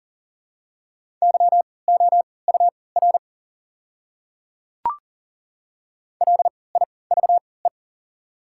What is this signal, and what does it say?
Morse code, heard as a keyed tone, about 39 wpm, spaced as for 22 wpm, 700 Hz. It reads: YOUR LIVE
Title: 3x Character Spacing - Top 300 Words - 40wpm